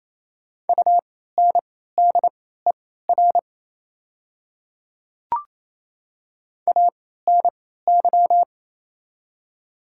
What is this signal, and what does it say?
Morse code, heard as a keyed tone, about 28 wpm, spaced as for 16 wpm, 700 Hz. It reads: UNDER ANY
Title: 3x Character Spacing - Top 200 Words - 28wpm